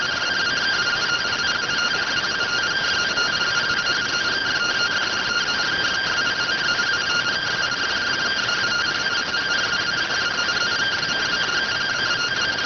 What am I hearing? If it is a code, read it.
Two-tone teleprinter signal, about 170 Hz shift.